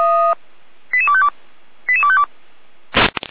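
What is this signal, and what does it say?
Five-tone selective call (ZVEI), heard as quick runs of five tones, tones of about 70 ms. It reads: 88252 88252
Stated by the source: Five-tone ZVEI selective call received on an SDR, from feuerwehr-orlamuende (fire brigade Orlamünde).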